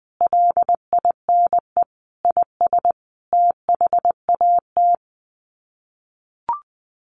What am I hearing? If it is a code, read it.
Morse code, heard as a keyed tone, about 20 wpm, 700 Hz. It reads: LINE IS THAT
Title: Sets of 5 Words Encore 20wpm